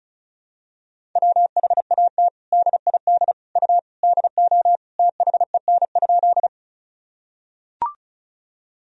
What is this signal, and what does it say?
Morse code, heard as a keyed tone, about 34 wpm, 700 Hz. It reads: WHAT DID U DO THEN?
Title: Sentences from Top 100 Words and Abbreviations 35wpm